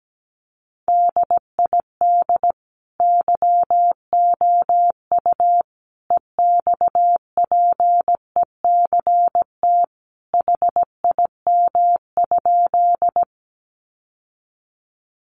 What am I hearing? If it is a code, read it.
Morse code, heard as a keyed tone, about 17 wpm, 700 Hz. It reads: DID YOU EXPECT HIM?
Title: Sentences from Top 1100 Words 17wpm